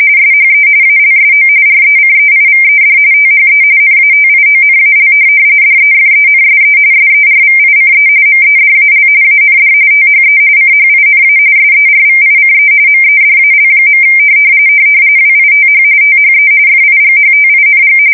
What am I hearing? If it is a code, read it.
RTTY, about 45.45 baud. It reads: SHE ALSO RECEIVED THE NATIONAL DEFENSE SERVICE MEDAL AND THE CHINA SERVICE MEDAL. WITH THE FIGHTING
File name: fsk170 / USS-New-Jersey